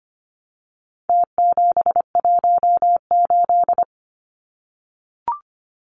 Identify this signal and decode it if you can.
Morse code, heard as a keyed tone, about 25 wpm, 700 Hz. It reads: T718
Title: Numbers - Contest Style - 4-Digit 25wpm